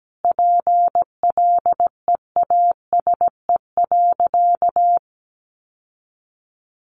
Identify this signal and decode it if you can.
Morse code, heard as a keyed tone, about 17 wpm, 700 Hz. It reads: PLEASE.
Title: Sentences from Top 1500 Words 17wpm